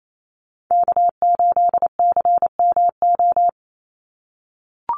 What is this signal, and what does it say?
Morse code, heard as a keyed tone, about 28 wpm, 700 Hz. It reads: K8CMO
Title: Call Signs - 28wpm